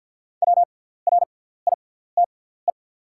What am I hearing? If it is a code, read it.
Morse code, heard as a keyed tone, about 48 wpm, spaced as for 17 wpm, 700 Hz. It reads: WRITE
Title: 6x Character Spacing - Top 500 Words - 50wpm